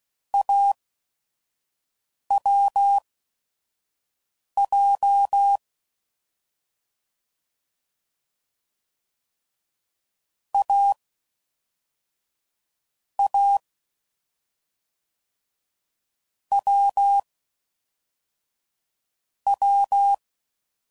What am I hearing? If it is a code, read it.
Morse code, heard as a keyed tone, about 16 wpm, spaced as for 5 wpm, 800 Hz. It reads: AWJ A A W W
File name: Combinations / Difficult Combos / awj1 ndb6 gz7